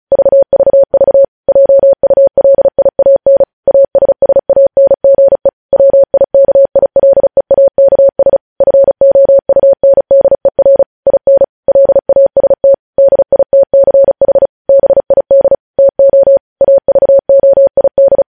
Morse code, heard as a keyed tone, about 36 wpm, 550 Hz. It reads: VVV JULIAN ASSANGE WIKILEAKS FOUNDER IN LAST DITCH BID TO AVOID